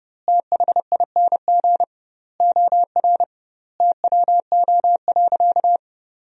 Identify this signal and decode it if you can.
Morse code, heard as a keyed tone, about 32 wpm, 700 Hz. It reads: THING OR TWO.
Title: 2x Word-Spacing - Sentences from Top 300 Words - 30wpm